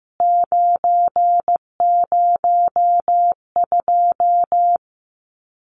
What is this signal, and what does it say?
Morse code, heard as a keyed tone, about 15 wpm, 700 Hz. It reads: 902
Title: CWT Exchange 15wpm